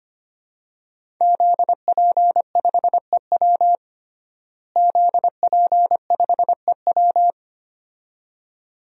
Morse code, heard as a keyed tone, about 25 wpm, 700 Hz. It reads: ZP5EW ZP5EW
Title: DXCC-100 Call Signs - 5 Characters - Repeated 2x - 25wpm